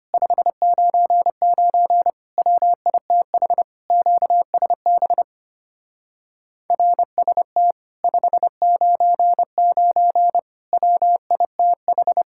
Morse code, heard as a keyed tone, about 32 wpm, 700 Hz. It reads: 599 WITH QSB RST 599 WITH